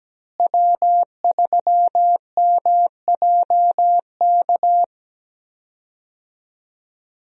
Morse code, heard as a keyed tone, about 17 wpm, 700 Hz. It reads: W3MJK
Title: USA Call Signs - 5 Characters - Repeated 2x - 17wpm